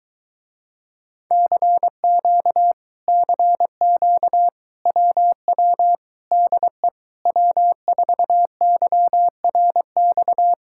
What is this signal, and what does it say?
Morse code, heard as a keyed tone, about 22 wpm, 700 Hz. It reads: CQ CQ WW DE W4YRX